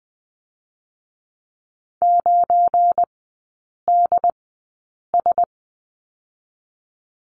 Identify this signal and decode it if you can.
Morse code, heard as a keyed tone, about 20 wpm, 700 Hz. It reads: 9 D S
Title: 212 Lesson 34e - Number 8 - Three Characters 20wpm